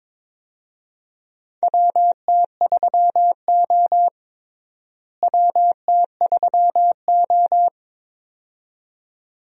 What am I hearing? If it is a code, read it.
Morse code, heard as a keyed tone, about 21 wpm, 700 Hz. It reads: WT3O WT3O
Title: USA Call Signs - 4 Characters - Repeated 2x - 22wpm